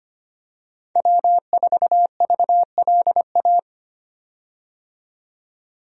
Morse code, heard as a keyed tone, about 25 wpm, 690 Hz. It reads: W4VLA
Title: USA Call Signs - 5 Characters - 25wpm